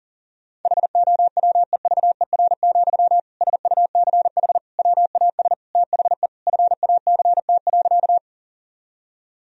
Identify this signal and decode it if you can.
Morse code, heard as a keyed tone, about 40 wpm, 700 Hz. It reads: HOWEVER, SUCH WAS THE FACT.
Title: Sentences from Top 1000 Words - Review Part 3 - 40wpm